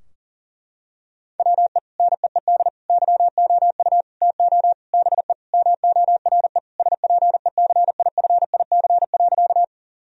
Morse code, heard as a keyed tone, about 37 wpm, 700 Hz. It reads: WE NEED YOU TO BE MORE SPECIFIC.
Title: Sentences from Top 1000 Words - Review Part 3 - 40wpm